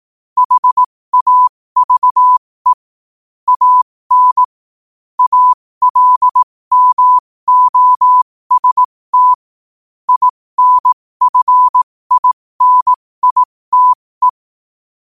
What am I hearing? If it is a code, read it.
Morse code, heard as a keyed tone, about 18 wpm, spaced as for 15 wpm, 1000 Hz. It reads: HAVE AN ALMOST INFINITE
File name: qotdfiles / 15wpm